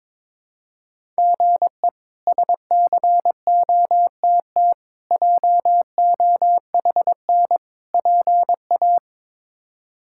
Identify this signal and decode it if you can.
Morse code, heard as a keyed tone, about 21 wpm, 700 Hz. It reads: GE SCOTT JOHN PA
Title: K1USN SST QSO Practice - 22wpm